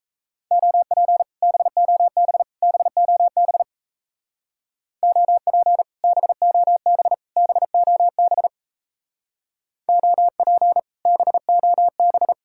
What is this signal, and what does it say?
Morse code, heard as a keyed tone, about 34 wpm, 700 Hz. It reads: OP BOB BOB OP BOB BOB OP BOB
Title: Speed-Racing Common QSO Phrases Encore - 28wpm